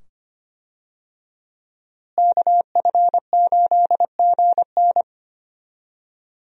Morse code, heard as a keyed tone, about 25 wpm, 700 Hz. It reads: KF8GN
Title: USA Call Signs - 5 Characters - Repeated 2x - 25wpm